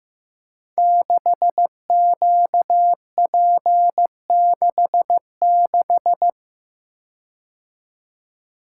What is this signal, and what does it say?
Morse code, heard as a keyed tone, about 15 wpm, 700 Hz. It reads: 6QP66
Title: Code Groups - 5 Letter:Numbers - 15wpm